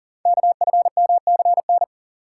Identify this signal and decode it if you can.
Morse code, heard as a keyed tone, about 40 wpm, 700 Hz. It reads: KFMCN
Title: Code Groups - 5 Letters - 40wpm